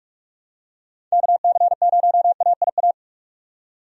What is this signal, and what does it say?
Morse code, heard as a keyed tone, about 46 wpm, 700 Hz. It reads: KC0AIA
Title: USA Call Signs - 6 Characters - 45wpm